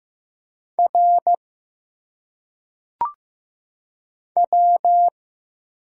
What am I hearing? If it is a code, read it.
Morse code, heard as a keyed tone, about 15 wpm, 700 Hz. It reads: R W
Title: Single Letters - Phonetic Pronunciation 15wpm